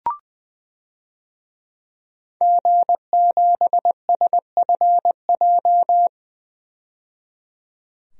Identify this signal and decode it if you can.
Morse code, heard as a keyed tone, about 21 wpm, 700 Hz. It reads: G7SFJ
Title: DXCC-100 Call Signs - 5 Characters - 20wpm